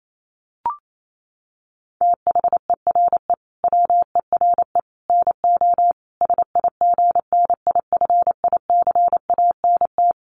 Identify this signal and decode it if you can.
Morse code, heard as a keyed tone, about 28 wpm, 700 Hz. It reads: THERE WERE NO SIGNIFICANT
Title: Sets of 5 Words Encore2 28wpm